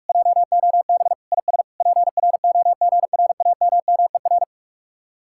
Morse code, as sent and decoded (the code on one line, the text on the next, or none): .--- --- -... / .. ... / .--. .-. --- --. .-. .- -- -- . .-.
JOB IS PROGRAMMER